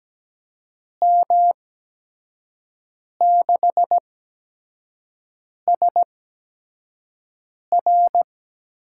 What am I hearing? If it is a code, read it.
Morse code, heard as a keyed tone, about 17 wpm, spaced as for 5 wpm, 700 Hz. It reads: M6SR
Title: USA Call Signs - 5 Characters - 8x Character Spacing - 17wpm